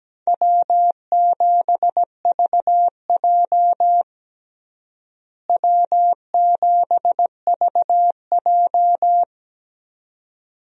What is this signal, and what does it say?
Morse code, heard as a keyed tone, about 17 wpm, 700 Hz. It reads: W7VJ W7VJ
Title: USA Call Signs - 4 Characters - Repeated 2x - 17wpm